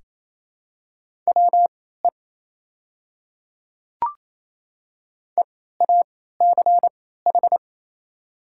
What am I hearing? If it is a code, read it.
Morse code, heard as a keyed tone, about 28 wpm, spaced as for 16 wpm, 700 Hz. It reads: WE EACH
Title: 3x Character Spacing - Top 100 Words - 28wpm